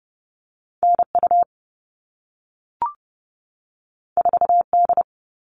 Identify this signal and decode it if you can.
Morse code, heard as a keyed tone, about 30 wpm, 700 Hz. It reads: NU 4D
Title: All Cut Numbers - 2-Digit 30wpm